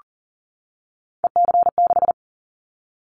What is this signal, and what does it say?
Morse code, heard as a keyed tone, about 38 wpm, 700 Hz. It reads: EC6
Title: Callsign Prefixes for Top 100 DXCC - 40wpm